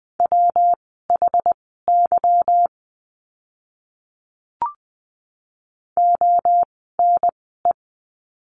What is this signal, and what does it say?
Morse code, heard as a keyed tone, about 20 wpm, spaced as for 14 wpm, 700 Hz. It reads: WHY ONE
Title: 230 Lesson 37f - Letter x - Words 20wpm